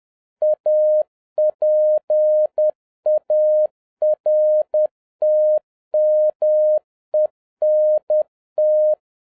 Morse code, heard as a keyed tone, about 10 wpm, 610 Hz. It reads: APARTMENT
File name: 2011-07-mp3 / n41 EXCITING QRP CW OPERATION 10wpm